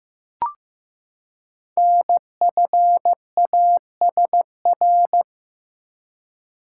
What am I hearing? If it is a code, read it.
Morse code, heard as a keyed tone, about 15 wpm, 700 Hz. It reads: NFASR
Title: Code Groups - 5 Letters - 15wpm